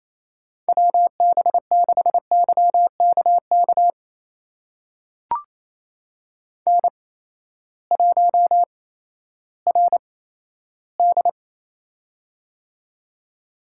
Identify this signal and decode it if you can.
Morse code, heard as a keyed tone, about 28 wpm, spaced as for 14 wpm, 700 Hz. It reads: WB6YKK N1RD
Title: USA Call Signs - 8x Character Spacing - 28wpm